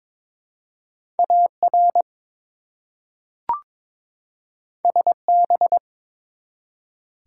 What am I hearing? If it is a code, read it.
Morse code, heard as a keyed tone, about 21 wpm, 700 Hz. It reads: AR SB
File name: ARRL Sections v2 - 22wpm